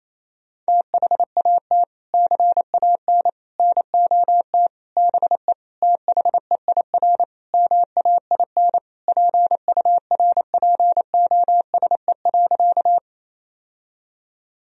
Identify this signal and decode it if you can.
Morse code, heard as a keyed tone, about 27 wpm, 700 Hz. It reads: THAT CAN NOT BE THEIR MAIN PURPOSE.
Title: Sentences from Top 1000 Words - Review Part 2 - 28wpm